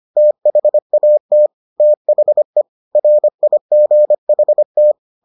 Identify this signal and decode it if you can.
Morse code, heard as a keyed tone, about 25 wpm, 600 Hz. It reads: THAT THE RIGHT